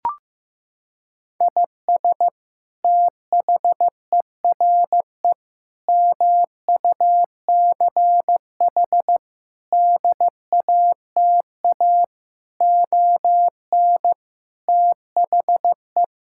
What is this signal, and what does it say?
Morse code, heard as a keyed tone, about 15 wpm, 700 Hz. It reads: IS THERE MUCH DATA ON THE
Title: Sentences from Top 1000 Words - Review Part 6 - 15wpm